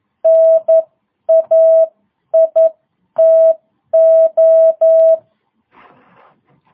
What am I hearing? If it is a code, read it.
Morse code, heard as a keyed tone, about 11 wpm, spaced as for 9 wpm, 660 Hz. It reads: NAITO